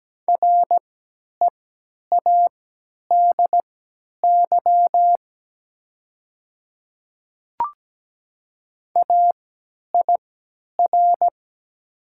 Morse code, heard as a keyed tone, about 17 wpm, spaced as for 10 wpm, 700 Hz. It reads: READY AIR
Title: 3x Character Spacing - Top 500 Words - 17wpm